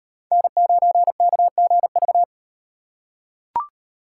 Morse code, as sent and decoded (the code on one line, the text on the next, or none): -. ----. -.- --. ...-
N9KGV